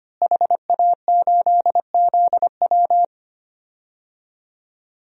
Morse code, heard as a keyed tone, about 25 wpm, 700 Hz. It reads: HA8ZW
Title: DXCC-100 Call Signs - 5 Characters - Repeated 2x - 25wpm